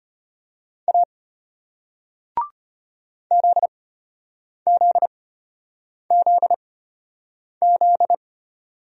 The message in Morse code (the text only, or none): A ZZZZ